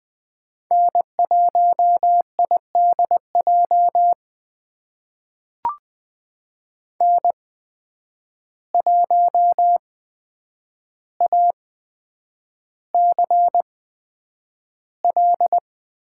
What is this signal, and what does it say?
Morse code, heard as a keyed tone, about 20 wpm, spaced as for 9 wpm, 700 Hz. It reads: N1IDJ N1ACL
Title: USA Call Signs - 8x Character Spacing - 20wpm